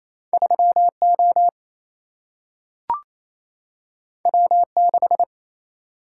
Morse code, heard as a keyed tone, about 28 wpm, 700 Hz. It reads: 3O W6
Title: Single-Letter Single-Number 28wpm